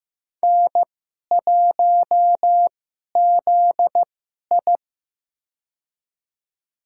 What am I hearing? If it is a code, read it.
Morse code, heard as a keyed tone, about 15 wpm, spaced as for 11 wpm, 700 Hz. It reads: N1ZI
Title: USA Call Signs - 2x Character Spacing - 15wpm